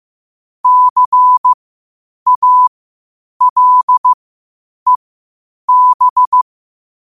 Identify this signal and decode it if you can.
Morse code, heard as a keyed tone, about 15 wpm, spaced as for 8 wpm, 1000 Hz. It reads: CALEB
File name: qotdfiles / 8wpm